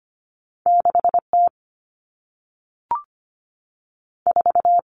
Morse code, heard as a keyed tone, about 25 wpm, 700 Hz. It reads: 6T 4
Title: Two Challenging Letter-Number Combinations 25wpm